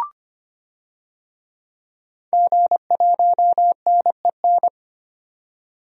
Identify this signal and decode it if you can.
Morse code, heard as a keyed tone, about 25 wpm, 700 Hz. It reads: G1NEN
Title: DXCC-100 Call Signs - 5 Characters - 25wpm